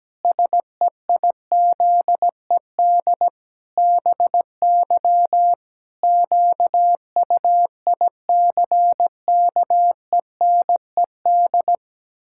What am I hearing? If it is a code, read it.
Morse code, heard as a keyed tone, about 17 wpm, 700 Hz. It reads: SEIZED BY QUICKENED
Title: Pangram Challenge v2 - 17wpm